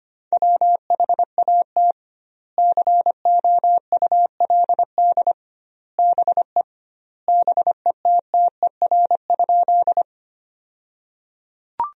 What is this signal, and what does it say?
Morse code, heard as a keyed tone, about 25 wpm, 700 Hz. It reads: WHAT COULD BE BETTER?
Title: 2x Word-Spacing - Sentences from Top 400 Words - 25wpm